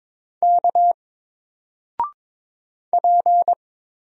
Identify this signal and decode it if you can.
Morse code, heard as a keyed tone, about 22 wpm, 700 Hz. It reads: K P
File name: Single-Letter-Number-22wpm